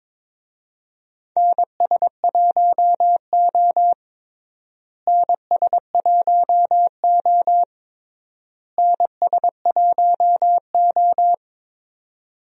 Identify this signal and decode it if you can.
Morse code, heard as a keyed tone, about 22 wpm, 700 Hz. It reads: NS1O NS1O NS1O
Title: USA Call Signs - 4 Characters - Repeated 3x - 22wpm